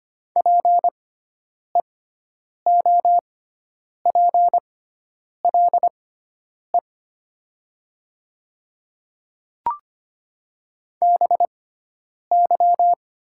Morse code, heard as a keyed tone, about 25 wpm, spaced as for 9 wpm, 700 Hz. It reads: PEOPLE BY
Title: 6x Character Spacing - Top 300 Words - 25wpm